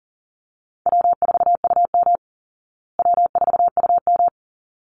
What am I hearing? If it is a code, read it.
Morse code, heard as a keyed tone, about 40 wpm, 700 Hz. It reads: W4UM W4UM
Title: Easy Call Signs Repeated 2x - 40wpm